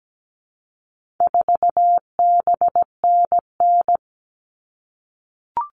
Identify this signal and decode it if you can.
Morse code, heard as a keyed tone, about 17 wpm, 700 Hz. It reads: 4BNN
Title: All Cut Numbers - 4-Digit 17wpm